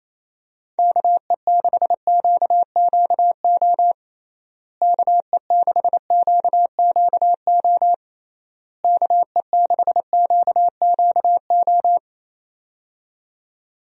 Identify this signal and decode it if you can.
Morse code, heard as a keyed tone, about 27 wpm, 700 Hz. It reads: KE6QQO KE6QQO KE6QQO
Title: USA Call Signs - 6 Characters - Repeated 3x - 28wpm